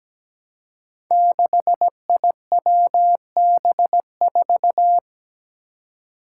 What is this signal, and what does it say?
Morse code, heard as a keyed tone, about 17 wpm, 700 Hz. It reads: 6IWB4
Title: Five Challenging Letter-Number Combinations 17wpm